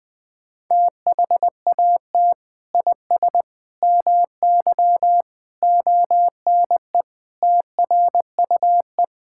Morse code, heard as a keyed tone, about 21 wpm, 700 Hz. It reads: THAT IS MY ONE TRUE
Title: Sentences from Top 1000 Words - Review Part 5 - 20wpm